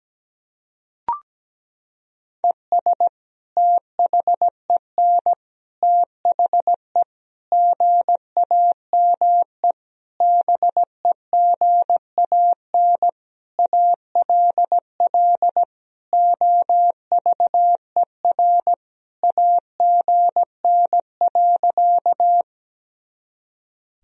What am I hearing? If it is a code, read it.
Morse code, heard as a keyed tone, about 17 wpm, 700 Hz. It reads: ES THEN THE GAME BEGAN ALL OVER AGN.
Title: Sentences from Top 500 Words and Abbreviations 17wpm